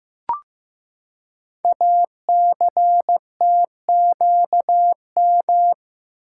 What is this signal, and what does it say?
Morse code, heard as a keyed tone, about 15 wpm, 700 Hz. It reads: ACTQM
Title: Code Groups - 5 Letters - 15wpm